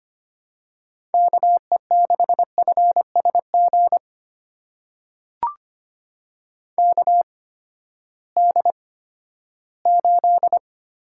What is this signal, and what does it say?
Morse code, heard as a keyed tone, about 25 wpm, spaced as for 14 wpm, 700 Hz. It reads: KE6FSG KD8
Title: USA Call Signs - 8x Character Spacing - 25wpm